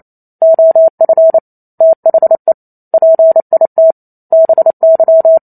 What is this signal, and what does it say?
Morse code, heard as a keyed tone, about 28 wpm, 650 Hz. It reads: OF THE PIT BY